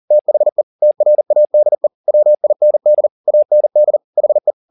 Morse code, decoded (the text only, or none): THE TRADE WIND AND HE